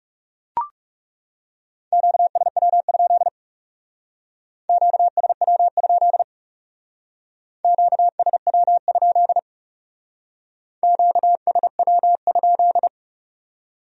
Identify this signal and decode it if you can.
Morse code, heard as a keyed tone, about 37 wpm, 700 Hz. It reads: QSW? QSW? QSW? QSW?
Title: Speed-Racing Q-Codes - 30wpm